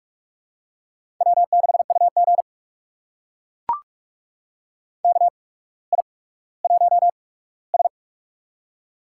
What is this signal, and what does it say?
Morse code, heard as a keyed tone, about 44 wpm, spaced as for 19 wpm, 690 Hz. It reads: W6UG KI1S
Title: USA Call Signs - 4 Characters - 8x Character Spacing - 45wpm